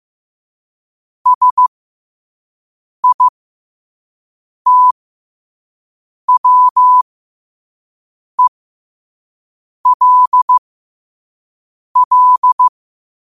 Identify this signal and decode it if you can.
Morse code, heard as a keyed tone, about 15 wpm, spaced as for 5 wpm, 1000 Hz. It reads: SITWELL
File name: qotdfiles / 5wpm